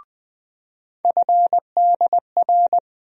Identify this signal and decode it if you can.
Morse code, heard as a keyed tone, about 20 wpm, 700 Hz. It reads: FDR
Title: USA Call Sign Postfixes - 3 Letters 20wpm